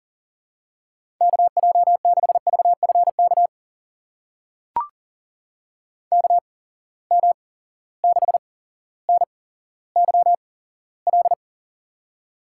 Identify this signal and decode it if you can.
Morse code, heard as a keyed tone, about 40 wpm, spaced as for 17 wpm, 700 Hz. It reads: KJ6VFK KM6NYL
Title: USA Call Signs - 6 Characters - 8x Character Spacing - 40wpm